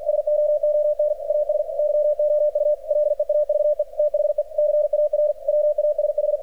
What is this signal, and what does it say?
Morse code, heard as a keyed tone, about 44 wpm, 600 Hz. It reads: SOON AS YOU LEAVE THE CAR WASH
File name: archived / CRSnet